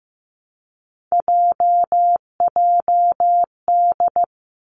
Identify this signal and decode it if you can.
Morse code, heard as a keyed tone, about 15 wpm, 700 Hz. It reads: JJD
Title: Three Challenging Letter-Number Combinations 15wpm